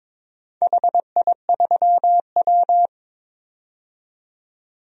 Morse code, heard as a keyed tone, about 22 wpm, 700 Hz. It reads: HI3W
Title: DXCC-100 Call Signs - 4 Characters - 22wpm